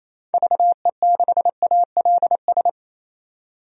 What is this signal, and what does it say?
Morse code, heard as a keyed tone, about 28 wpm, 700 Hz. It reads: VE6ALS
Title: DXCC-100 Call Signs - 6 Characters - 28wpm